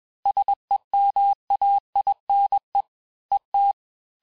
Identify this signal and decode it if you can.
Morse code, heard as a keyed tone, about 21 wpm, 800 Hz. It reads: SEMAINE ET
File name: teste-FIN-de-QSO-vitesse-20mpm